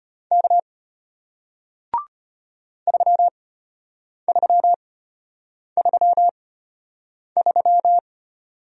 Morse code, heard as a keyed tone, about 33 wpm, spaced as for 8 wpm, 700 Hz. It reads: K 3333